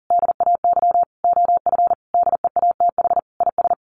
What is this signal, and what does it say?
Morse code, heard as a keyed tone, about 40 wpm, 700 Hz. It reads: DAY OF DEATH IS
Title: Sentences from Top 1100 Words 40wpm